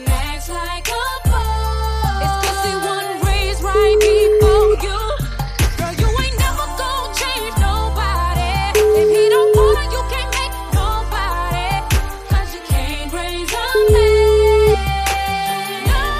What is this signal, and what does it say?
Ringback tone (Europe, 425 Hz), with noise.